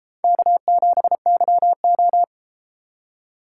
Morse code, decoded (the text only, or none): K7YO